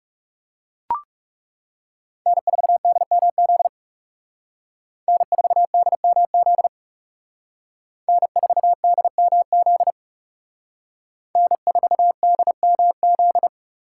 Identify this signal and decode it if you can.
Morse code, heard as a keyed tone, about 36 wpm, 700 Hz. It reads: N4DMZ N4DMZ N4DMZ N4DMZ
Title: Speed Racing - USA 1x3 Call Signs 30wpm